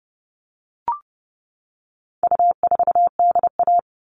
Morse code, heard as a keyed tone, about 30 wpm, 700 Hz. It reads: U4DA
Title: All Cut Numbers - 4-Digit 30wpm